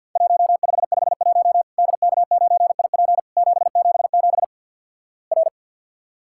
Morse code, heard as a keyed tone, about 50 wpm, 700 Hz. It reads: 1551 DK9IP 777 R
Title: Worked All Europe DX Contest - QTC Exchange with Repeats - 50wpm